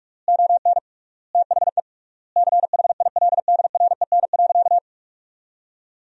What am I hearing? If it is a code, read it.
Morse code, heard as a keyed tone, about 46 wpm, 700 Hz. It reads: ON THE CHILDREN.
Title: 3x Word-Spacing - Sentences from Top 300 Words - 45wpm